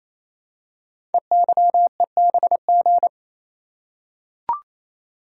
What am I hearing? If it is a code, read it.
Morse code, heard as a keyed tone, about 29 wpm, 700 Hz. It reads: EYEBG
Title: Code Groups - 5 Letters - 28wpm